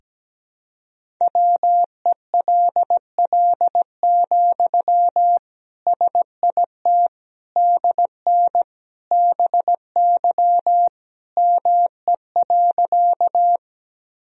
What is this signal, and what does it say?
Morse code, heard as a keyed tone, about 17 wpm, 700 Hz. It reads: WELL, SIT DN BY ME.
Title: Sentences from Top 500 Words and Abbreviations 17wpm